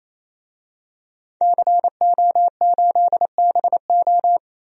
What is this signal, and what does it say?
Morse code, heard as a keyed tone, about 28 wpm, 700 Hz. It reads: CO8BO
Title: DXCC-100 Call Signs - 5 Characters - 28wpm